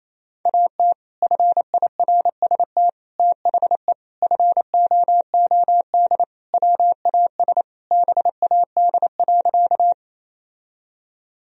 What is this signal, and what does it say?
Morse code, heard as a keyed tone, about 28 wpm, 700 Hz. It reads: AT FIRST THE FOOD WAS BAD.
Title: Sentences from Top 1000 Words - Review Part 1 - 28wpm